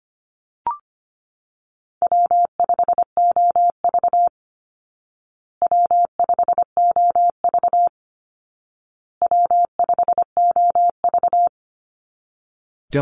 Morse code, heard as a keyed tone, about 25 wpm, 700 Hz. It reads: W5OV W5OV W5OV
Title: Easy Call Signs Repeated 3x - 25wpm